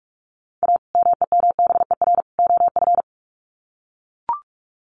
Morse code, heard as a keyed tone, about 44 wpm, 700 Hz. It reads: A MEMBER OF
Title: Sets of 3 Words 45wpm